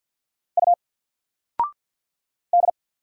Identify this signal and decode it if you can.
Morse code, heard as a keyed tone, about 53 wpm, 700 Hz. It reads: U D
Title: Single Letter-Number 50wpm